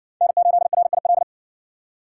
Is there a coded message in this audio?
Yes: Morse code from 0.2 to 1.2 s, heard as a keyed tone, about 60 wpm, 700 Hz. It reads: N8RIL